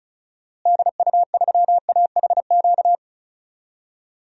Morse code, heard as a keyed tone, about 36 wpm, 700 Hz. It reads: DU3AHQ
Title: DXCC-100 Call Signs - 6 Characters - Repeated 2x - 35wpm